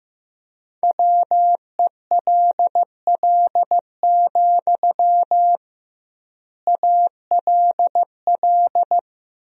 Morse code, heard as a keyed tone, about 15 wpm, 700 Hz. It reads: WELL, ALL
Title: 2x Word-Spacing - Sentences from Top 200 Words - 15wpm